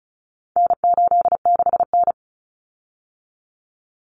Morse code, heard as a keyed tone, about 34 wpm, 700 Hz. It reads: N86N
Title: Common Cut Numbers - 4-Digit 35wpm